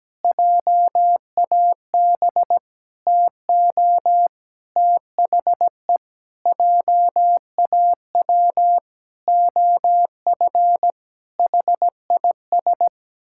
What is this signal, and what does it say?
Morse code, heard as a keyed tone, about 17 wpm, 700 Hz. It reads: JAB TO THE JAW OF HIS
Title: Pangram Challenge v2 - 17wpm